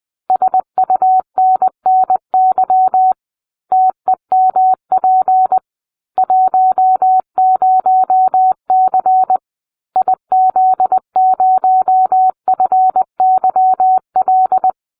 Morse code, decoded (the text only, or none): SUNNY TEMP 10C IZ0FYL